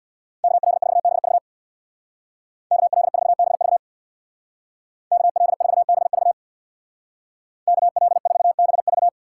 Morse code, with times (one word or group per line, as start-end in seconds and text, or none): 0.4-1.4 KL4BV
2.7-3.8 KL4BV
5.1-6.3 KL4BV
7.7-9.1 KL4BV